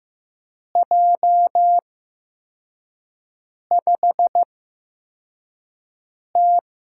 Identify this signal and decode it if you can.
Morse code, heard as a keyed tone, about 15 wpm, spaced as for 4 wpm, 700 Hz. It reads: J5T
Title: USA Call Signs - 5 Characters - 8x Character Spacing - 15wpm